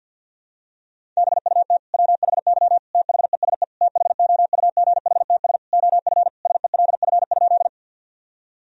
Morse code, heard as a keyed tone, about 51 wpm, 700 Hz. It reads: BUT WHY THESE THOUGHTS OF SELF?
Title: Sentences from Top 1000 Words - Review Part 1 - 50wpm